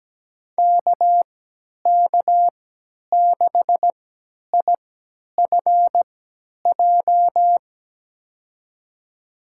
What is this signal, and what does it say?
Morse code, heard as a keyed tone, about 17 wpm, spaced as for 10 wpm, 700 Hz. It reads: KK6IFJ
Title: USA Call Signs - 3x Character Spacing - 17wpm